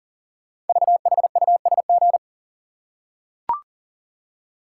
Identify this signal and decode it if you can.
Morse code, heard as a keyed tone, about 41 wpm, 690 Hz. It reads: VHUSG
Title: Code Groups - 5 Letters - 40wpm